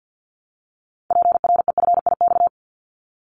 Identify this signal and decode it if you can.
Morse code, heard as a keyed tone, about 50 wpm, 700 Hz. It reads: PREFIX
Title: Callsign Prefixes for Top 100 DXCC - 50wpm